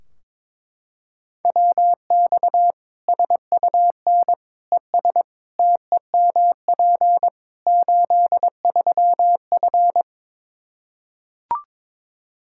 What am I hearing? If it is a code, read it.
Morse code, heard as a keyed tone, about 21 wpm, 700 Hz. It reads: WX SUN ES TEMP 83F
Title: K1USN SST QSO Practice 22wpm